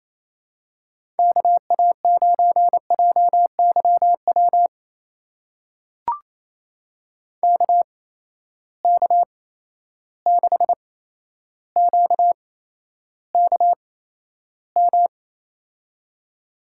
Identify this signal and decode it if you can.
Morse code, heard as a keyed tone, about 28 wpm, spaced as for 12 wpm, 700 Hz. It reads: KA9JYW KK6QKM
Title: USA Call Signs - 8x Character Spacing - 28wpm